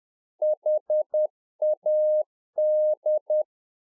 Morse code, heard as a keyed tone, about 10 wpm, 610 Hz. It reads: HAD
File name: 2011-04-mp3 / A SMALL TRANCEIVER 1 10WPM